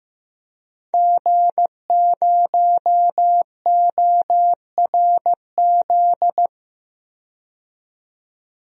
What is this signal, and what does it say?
Morse code, heard as a keyed tone, about 15 wpm, 700 Hz. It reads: G0ORZ
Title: DXCC-100 Call Signs - 5 Characters - 15wpm